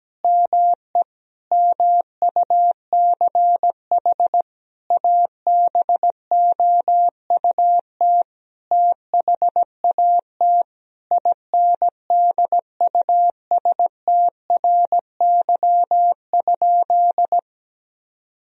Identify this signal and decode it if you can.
Morse code, heard as a keyed tone, about 17 wpm, 700 Hz. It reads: ME MUCH ABOUT THAT INDUSTRY?
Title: Sentences from Top 1000 Words - Review Part 4 - 17wpm